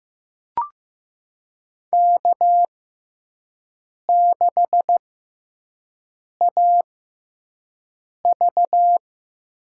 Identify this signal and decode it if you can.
Morse code, heard as a keyed tone, about 15 wpm, spaced as for 5 wpm, 700 Hz. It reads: K6AV
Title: USA Call Signs - 6x Character Spacing - 15wpm